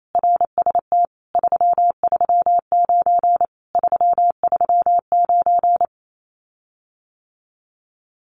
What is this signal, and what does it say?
Morse code, heard as a keyed tone, about 28 wpm, 700 Hz. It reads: RST 339 339
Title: Common QSO Phrases Encore 28wpm